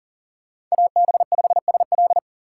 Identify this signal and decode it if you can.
Morse code, heard as a keyed tone, about 40 wpm, 700 Hz. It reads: AB5SL
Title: USA Call Signs - 5 Characters - 2x Character Spacing - 40wpm